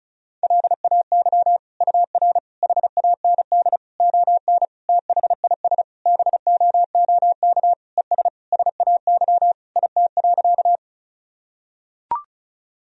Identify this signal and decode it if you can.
Morse code, heard as a keyed tone, about 36 wpm, 700 Hz. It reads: LAY UR HAND ON THIS BOOK ES SAY IT.
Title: Sentences from Top 400 Words and Abbreviations 35wpm